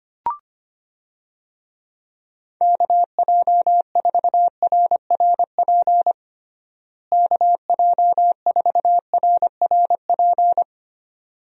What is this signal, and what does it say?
Morse code, heard as a keyed tone, about 25 wpm, 700 Hz. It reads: KJ4RRP KJ4RRP
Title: USA Call Signs - 6 Characters - Repeated 2x - 25wpm